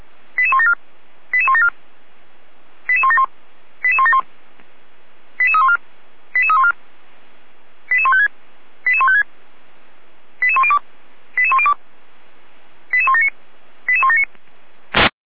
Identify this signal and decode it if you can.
Five-tone selective call (ZVEI), heard as quick runs of five tones, tones of about 70 ms. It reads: 88174 88174 88171 88171 88314 88314 88156 88156 88102 88102 88179 88179